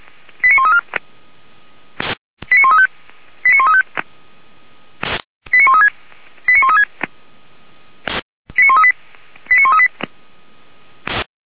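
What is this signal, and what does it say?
Five-tone selective call (ZVEI), heard as quick runs of five tones, tones of about 80 ms. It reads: 89134 89135 89135 89136 89136 89138 89138
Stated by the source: ZVEI five-tone selective call received on SDR, from fire brigade feuerwehr-orlamuende.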